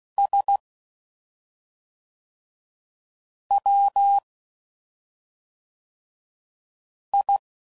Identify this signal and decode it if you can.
Morse code, heard as a keyed tone, about 16 wpm, 800 Hz. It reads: S W I
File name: Suffixes / Special ING's / SlowToFaster ING's